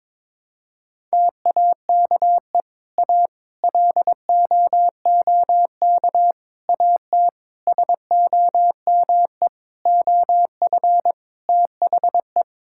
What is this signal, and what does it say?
Morse code, heard as a keyed tone, about 22 wpm, 700 Hz. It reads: TAKE A LOOK AT SOME OF THE
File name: Sentences with 7-Word Phrases - 22wpm